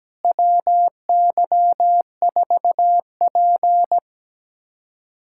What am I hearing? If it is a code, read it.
Morse code, heard as a keyed tone, about 17 wpm, 700 Hz. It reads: WY4P
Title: USA Callsign DE Callsign - 4 Characters - 17wpm